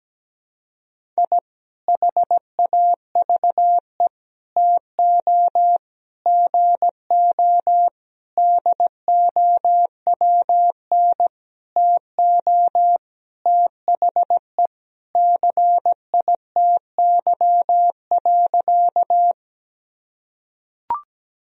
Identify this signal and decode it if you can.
Morse code, heard as a keyed tone, about 17 wpm, 700 Hz. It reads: I HAVE TO GO DOWN TO THE CITY.
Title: Sentences from Top 1000 Words - Review Part 1 - 17wpm